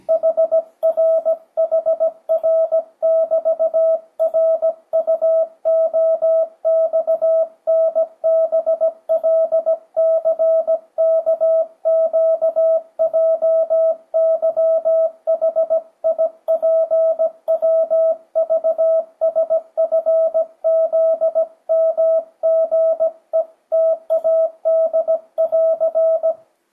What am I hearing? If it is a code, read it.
Morse code, heard as a keyed tone, about 17 wpm, 650 Hz. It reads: HRHR<BT>RUOXNBLCKQJYHIPWVSFZMGETAD<AR>
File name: 2025cw / speedup / 17wpm